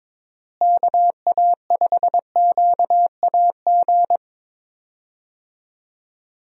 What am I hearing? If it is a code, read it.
Morse code, heard as a keyed tone, about 22 wpm, 700 Hz. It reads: KA5QAG
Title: USA Call Signs - 6 Characters - Repeated 3x - 22wpm